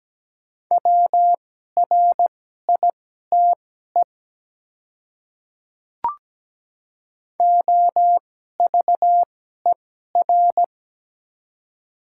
Morse code, heard as a keyed tone, about 17 wpm, spaced as for 12 wpm, 700 Hz. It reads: WRITE OVER
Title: 2x Character Spacing - Top 300 Words - 17wpm